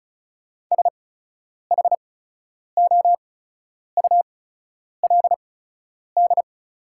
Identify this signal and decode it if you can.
Morse code, heard as a keyed tone, about 36 wpm, spaced as for 10 wpm, 690 Hz. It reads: SHOULD
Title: 8x Character Spacing - Top 500 Words - 35wpm